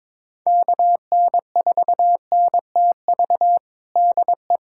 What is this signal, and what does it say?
Morse code, heard as a keyed tone, about 22 wpm, 700 Hz. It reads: KN4NTV DE
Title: USA Callsign DE Callsign 22wpm